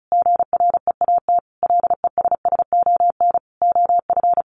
Morse code, heard as a keyed tone, about 34 wpm, 700 Hz. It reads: GREAT LESSON OF
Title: Sentences from Top 1200 Words 35wpm